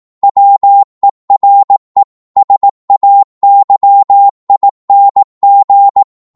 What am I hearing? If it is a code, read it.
Morse code, heard as a keyed tone, about 18 wpm, 800 Hz. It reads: WERE SAYING